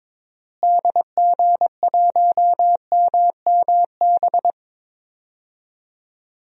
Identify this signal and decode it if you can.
Morse code, heard as a keyed tone, about 22 wpm, 700 Hz. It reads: DG1MMB
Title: DXCC-100 Call Signs - 6 Characters - Repeated 2x - 22wpm